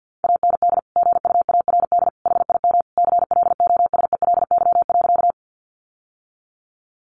Morse code, heard as a keyed tone, about 49 wpm, 700 Hz. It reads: AND GUARD HIM CLOSELY.
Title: Sentences from Top 1400 Words 50wpm